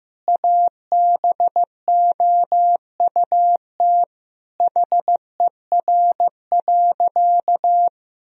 Morse code, heard as a keyed tone, about 15 wpm, 700 Hz. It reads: ABOUT HER.
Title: Sentences from Top 1000 Words - Review Part 4 - 15wpm